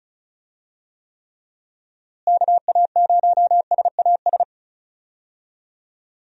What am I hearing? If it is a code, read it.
Morse code, heard as a keyed tone, about 36 wpm, 700 Hz. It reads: KA0SAS
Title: USA Call Signs - 6 Characters - Repeated 3x - 35wpm